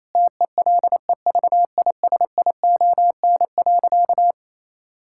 Morse code, heard as a keyed tone, about 28 wpm, 700 Hz. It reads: TELEVISION.
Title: Sentences from Top 1000 Words - Review Part 5 - 28wpm